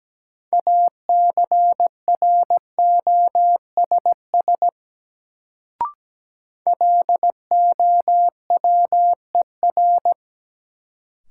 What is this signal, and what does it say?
Morse code, heard as a keyed tone, about 17 wpm, 700 Hz. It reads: ACROSS LOWER
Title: Sentences from Top 700 Words - New Word Drill - 17wpm